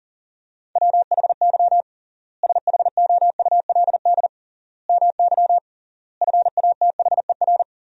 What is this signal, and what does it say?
Morse code, heard as a keyed tone, about 40 wpm, 700 Hz. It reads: WHY SHOULD MY FATHER
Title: 3x Word-Spacing - Sentences from Top 500 Words - 40wpm